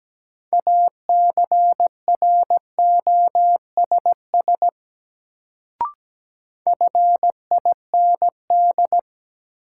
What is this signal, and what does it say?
Morse code, heard as a keyed tone, about 17 wpm, 700 Hz. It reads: ACROSS FIND
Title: Sets of 2 Words - New Word Drill - 17wpm